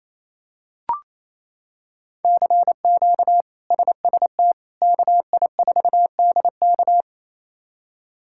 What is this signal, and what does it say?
Morse code, heard as a keyed tone, about 28 wpm, 700 Hz. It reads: CQ SST KI4DK
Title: K1USN SST QSO Practice - 28wpm